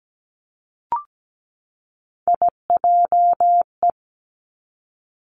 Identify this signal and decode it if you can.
Morse code, heard as a keyed tone, about 17 wpm, 700 Hz. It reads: IJE
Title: Three Challenging Letter-Number Combinations 17wpm